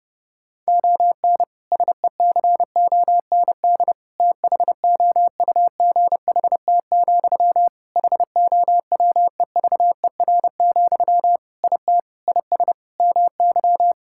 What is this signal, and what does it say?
Morse code, heard as a keyed tone, about 30 wpm, 700 Hz. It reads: ON SECOND THOUGHT, HOWEVER, IT IS MY
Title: Sentences from Top 1600 Words 30wpm